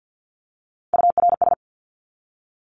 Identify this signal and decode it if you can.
Morse code, heard as a keyed tone, about 50 wpm, 700 Hz. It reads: URS
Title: Abbreviations 50wpm